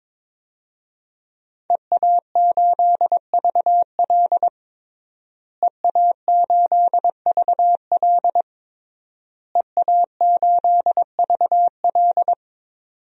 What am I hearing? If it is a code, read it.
Morse code, heard as a keyed tone, about 22 wpm, 700 Hz. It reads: EA8VL EA8VL EA8VL
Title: DXCC-100 Call Signs - 5 Characters - Repeated 3x - 22wpm